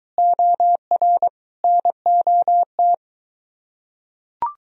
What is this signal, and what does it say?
Morse code, heard as a keyed tone, about 23 wpm, 700 Hz. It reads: OR NOT